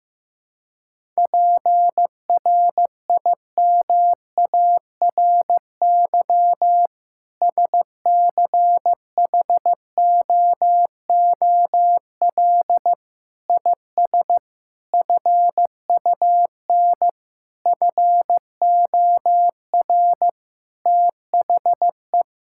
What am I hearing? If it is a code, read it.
Morse code, heard as a keyed tone, about 15 wpm, 700 Hz. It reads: PRIMARY SCHOOL IS FUN FOR THE
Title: Sentences from Top 1000 Words - Review Part 5 - 15wpm